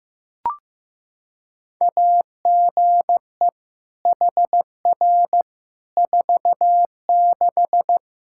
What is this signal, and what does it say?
Morse code, heard as a keyed tone, about 15 wpm, 700 Hz. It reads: AGE HR 46
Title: Practice QSO - Exchange 3 - More Info - 15wpm